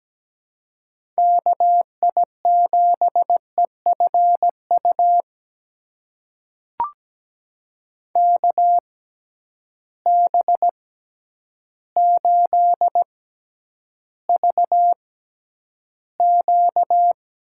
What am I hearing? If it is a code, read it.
Morse code, heard as a keyed tone, about 17 wpm, spaced as for 9 wpm, 700 Hz. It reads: KI7EFU KB8VQ
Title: USA Call Signs - 6 Characters - 6x Character Spacing - 17wpm